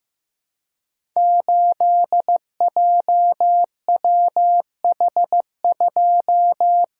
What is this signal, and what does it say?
Morse code, heard as a keyed tone, about 15 wpm, 700 Hz. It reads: 8JWH2
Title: Code Groups - 5 Letter:Numbers - 15wpm